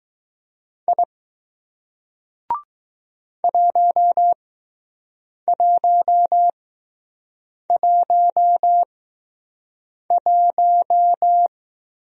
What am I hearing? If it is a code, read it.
Morse code, heard as a keyed tone, about 20 wpm, spaced as for 6 wpm, 700 Hz. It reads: I 1111